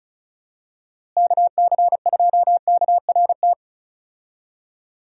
Morse code, heard as a keyed tone, about 36 wpm, 700 Hz. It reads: KC2KRT
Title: USA Call Signs - 6 Characters - Repeated 2x - 35wpm